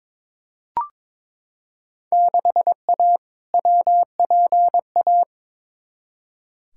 Morse code, heard as a keyed tone, about 22 wpm, 700 Hz. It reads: 6A WPA
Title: Field Day Exchanges Encore - 22wpm